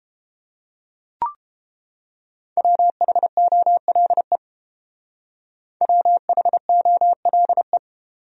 Morse code, heard as a keyed tone, about 29 wpm, 700 Hz. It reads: WHOLE WHOLE